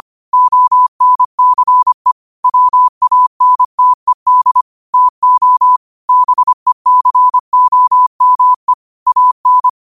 Morse code, heard as a keyed tone, about 25 wpm, 1000 Hz. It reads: ONCE WANTED TO BECOME AN
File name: qotdfiles / 25wpm